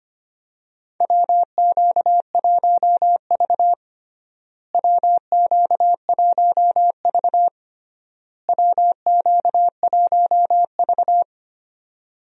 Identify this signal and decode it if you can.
Morse code, heard as a keyed tone, about 25 wpm, 700 Hz. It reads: WQ1V WQ1V WQ1V
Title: USA Call Signs - 4 Characters - Repeated 3x - 25wpm